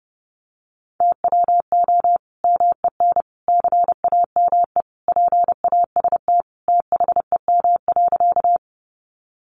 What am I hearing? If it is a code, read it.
Morse code, heard as a keyed tone, about 29 wpm, 700 Hz. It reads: TWO MEN CAME PAST THEM.
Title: Sentences from Top 600 Words 30wpm